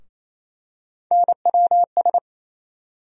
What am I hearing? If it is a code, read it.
Morse code, heard as a keyed tone, about 28 wpm, 700 Hz. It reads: NWS
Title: USA Call Sign Postfixes - 3 Letters 28wpm